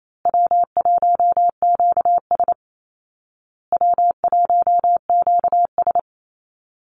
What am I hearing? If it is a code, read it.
Morse code, heard as a keyed tone, about 28 wpm, 700 Hz. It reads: W1QS W1QS
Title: Easy Call Signs Repeated 3x - 28wpm